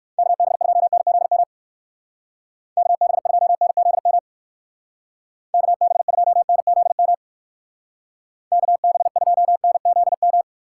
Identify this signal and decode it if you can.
Morse code, heard as a keyed tone, about 54 wpm, 700 Hz. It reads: KB2NZM KB2NZM KB2NZM KB2NZM
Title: Speed Racing - USA 2x3 Call Signs 45wpm